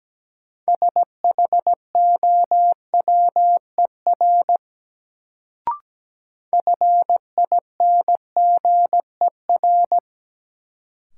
Morse code, heard as a keyed tone, about 17 wpm, 700 Hz. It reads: SHOWER FINGER
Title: Sentences from Top 1600 Words - New Word Drill - 17wpm